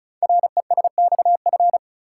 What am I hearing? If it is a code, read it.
Morse code, heard as a keyed tone, about 36 wpm, 700 Hz. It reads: RESXF
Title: Code Groups - 5 Letters - 35wpm